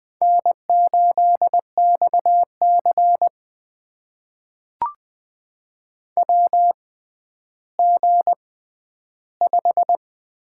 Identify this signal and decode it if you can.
Morse code, heard as a keyed tone, about 20 wpm, spaced as for 11 wpm, 700 Hz. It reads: N8XC WG5
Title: USA Call Signs - 6x Character Spacing - 20wpm